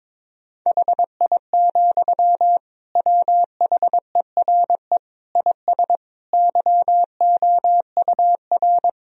Morse code, heard as a keyed tone, about 21 wpm, 700 Hz. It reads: HI, WHERE IS YOUR
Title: Sentences from Top 1000 Words - Review Part 3 - 22wpm